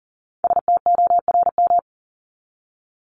Morse code, heard as a keyed tone, about 40 wpm, 700 Hz. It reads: STORM
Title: Top 1251-1500 Words 40wpm